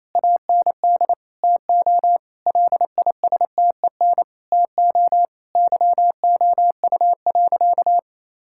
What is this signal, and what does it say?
Morse code, heard as a keyed tone, about 28 wpm, 700 Hz. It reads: AND TO LISTEN TO YOU.
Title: Sentences from Top 1600 Words 28wpm